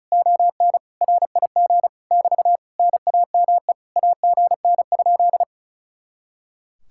Morse code, heard as a keyed tone, about 43 wpm, 700 Hz. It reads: ON RIG <BT> NAME AGN?
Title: Speed-Racing Common QSO Phrases - 35wpm